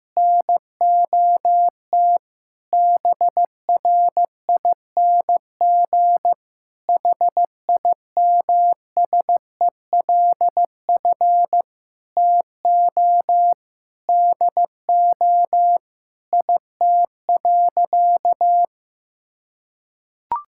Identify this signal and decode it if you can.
Morse code, heard as a keyed tone, about 15 wpm, 700 Hz. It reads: NOT BRING HIMSELF TO DO IT.
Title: Sentences from Top 1600 Words 15wpm